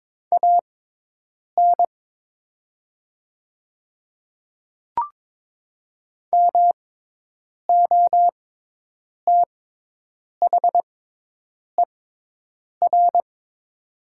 Morse code, heard as a keyed tone, about 22 wpm, spaced as for 8 wpm, 700 Hz. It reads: AN MOTHER
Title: 6x Character Spacing - Top 200 Words - 22wpm